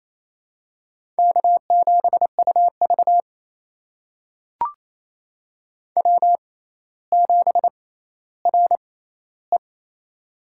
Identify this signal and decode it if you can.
Morse code, heard as a keyed tone, about 28 wpm, spaced as for 14 wpm, 700 Hz. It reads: K7UV W7RE
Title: USA Call Signs - 4 Characters - 6x Character Spacing - 28wpm